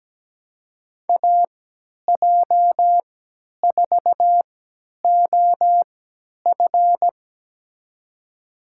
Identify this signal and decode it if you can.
Morse code, heard as a keyed tone, about 17 wpm, spaced as for 10 wpm, 700 Hz. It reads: AJ4OF
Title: USA Call Signs - 5 Characters - 3x Character Spacing - 17wpm